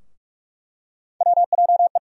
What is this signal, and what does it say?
Morse code, heard as a keyed tone, about 46 wpm, 700 Hz. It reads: WJE
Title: USA Call Sign Postfixes - 3 Letters 45wpm